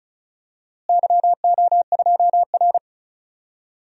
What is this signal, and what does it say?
Morse code, heard as a keyed tone, about 34 wpm, 700 Hz. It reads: YO2R
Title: DXCC-100 Call Signs - 4 Characters - Repeated 3x - 35wpm